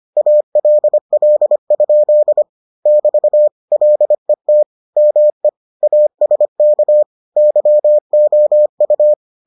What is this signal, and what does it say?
Morse code, heard as a keyed tone, about 25 wpm, 600 Hz. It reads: ALL? <BT> LET ME ASK YOU